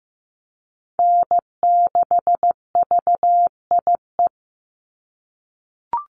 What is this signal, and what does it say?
Morse code, heard as a keyed tone, about 15 wpm, 700 Hz. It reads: N6VIE
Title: USA Call Signs 15wpm